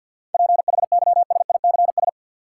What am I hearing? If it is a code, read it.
Morse code, heard as a keyed tone, about 53 wpm, 700 Hz. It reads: PHYSICS
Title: Sentences from Top 1500 Words - New Word Drill Alt - 50wpm